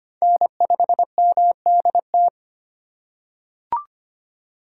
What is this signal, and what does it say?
Morse code, heard as a keyed tone, about 25 wpm, 700 Hz. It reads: N5MDT
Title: USA Call Signs - 5 Characters - 4x Character Spacing - 25wpm